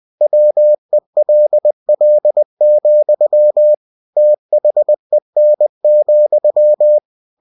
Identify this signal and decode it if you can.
Morse code, heard as a keyed tone, about 20 wpm, 600 Hz. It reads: WELL, THEN,
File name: cw-books / winnie-the-pooh / winnie-20wpm-page-0015